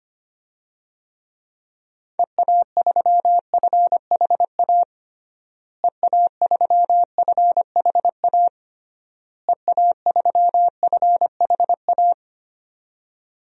Morse code, heard as a keyed tone, about 25 wpm, 700 Hz. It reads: EA3FHA EA3FHA EA3FHA
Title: DXCC-100 Call Signs - 6 Characters - Repeated 3x - 25wpm